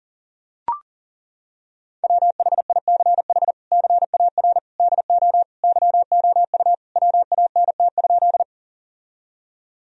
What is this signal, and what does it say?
Morse code, heard as a keyed tone, about 40 wpm, 700 Hz. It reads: WHICH CAR DO YOU WANT?
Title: Sentences from Top 1000 Words - Review Part 1 - 40wpm